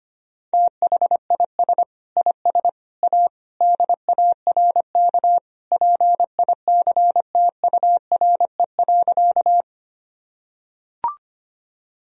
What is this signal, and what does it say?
Morse code, heard as a keyed tone, about 25 wpm, 700 Hz. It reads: THIS IS A DARK PICTURE.
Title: Sentences from Top 1000 Words - Review Part 1 - 25wpm